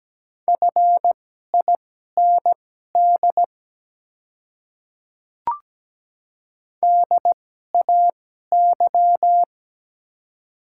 Morse code, heard as a keyed tone, about 17 wpm, spaced as for 12 wpm, 690 Hz. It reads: FIND DAY
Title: 2x Character Spacing - Top 100 Words - 17wpm